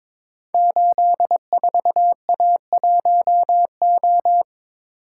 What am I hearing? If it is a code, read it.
Morse code, heard as a keyed tone, about 22 wpm, 700 Hz. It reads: 84A1O
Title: Code Groups - 5 Letter:Numbers - 22wpm